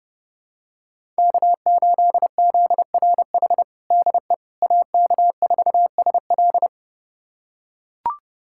Morse code, heard as a keyed tone, about 30 wpm, 700 Hz. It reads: K8ZRH DE AK4SL
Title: USA Callsign DE Callsign - 5 Characters - 30wpm